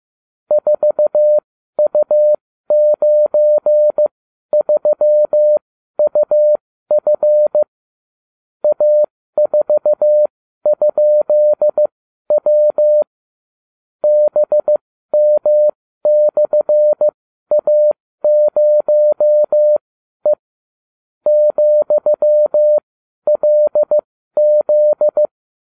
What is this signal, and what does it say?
Morse code, heard as a keyed tone, about 15 wpm, spaced as for 12 wpm, 600 Hz. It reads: 4U93UF A4?W BM/A0E ,LZ